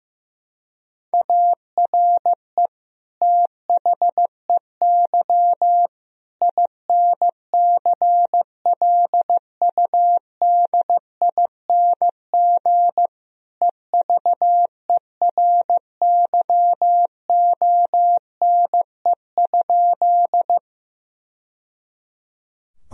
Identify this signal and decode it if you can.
Morse code, heard as a keyed tone, about 15 wpm, 700 Hz. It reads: ARE THEY INCLUDING EVERYONE?
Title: Sentences from Top 1000 Words - Review Part 4 - 15wpm